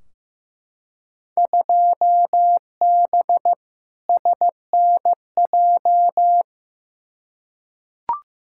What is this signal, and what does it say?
Morse code, heard as a keyed tone, about 15 wpm, 700 Hz. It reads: 2B SNJ
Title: Field Day Exchanges Encore - 15wpm